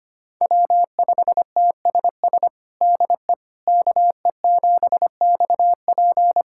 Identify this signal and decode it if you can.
Morse code, heard as a keyed tone, about 25 wpm, 700 Hz. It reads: W5TSS DE KE7XP
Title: USA Callsign DE Callsign 25wpm